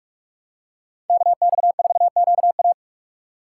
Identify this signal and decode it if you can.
Morse code, heard as a keyed tone, about 43 wpm, 700 Hz. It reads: KX4QA